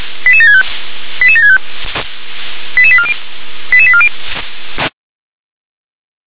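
Five-tone selective call (ZVEI), heard as quick runs of five tones, tones of about 70 ms. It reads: 88765 88765 88840 88840
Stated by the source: ZVEI five-tone selective call received with an SDR.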